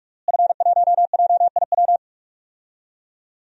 Morse code, heard as a keyed tone, about 46 wpm, 700 Hz. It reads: F1JIW
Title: DXCC-100 Call Signs - 5 Characters - Repeated 3x - 45wpm